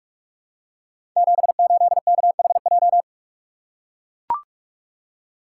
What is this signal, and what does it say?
Morse code, heard as a keyed tone, about 44 wpm, 700 Hz. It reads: 78KHJ
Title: Code Groups - 5 Letter:Numbers - 45wpm